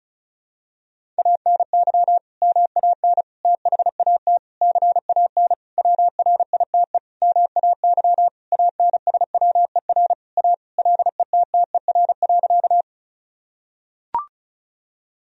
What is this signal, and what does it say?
Morse code, heard as a keyed tone, about 36 wpm, 700 Hz. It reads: ANY MAN THAT CAN WRITE MAY ANSWER A LETTER.
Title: Sentences from Top 1000 Words - Review Part 1 - 35wpm